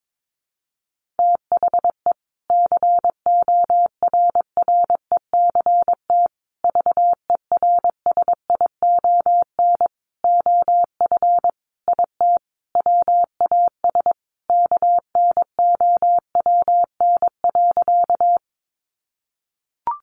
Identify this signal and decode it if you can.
Morse code, heard as a keyed tone, about 22 wpm, 700 Hz. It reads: THE CORRECT VERSION OF IT WAS KNOWN.
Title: Sentences from Top 1400 Words 22wpm